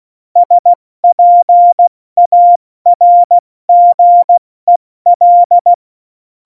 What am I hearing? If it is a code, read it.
Morse code, heard as a keyed tone, about 16 wpm, 700 Hz. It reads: SPARGEL
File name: Morsekurs / Witze / Witz 16. April 2019, 70 F80